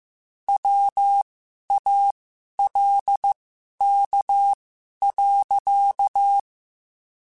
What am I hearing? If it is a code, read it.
Morse code, heard as a keyed tone, about 15 wpm, spaced as for 11 wpm, 800 Hz. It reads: WALK.
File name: Quotes / Quotes-mp3s / Temper @13